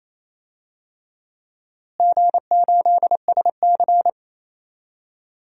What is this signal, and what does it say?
Morse code, heard as a keyed tone, about 28 wpm, 700 Hz. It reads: G8SC
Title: DXCC-100 Call Signs - 4 Characters - Repeated 3x - 28wpm